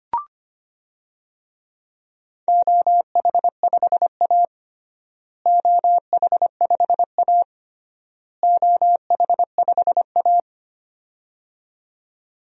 Morse code, heard as a keyed tone, about 25 wpm, 700 Hz. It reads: OH5A OH5A OH5A
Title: DXCC-100 Call Signs - 4 Characters - Repeated 3x - 25wpm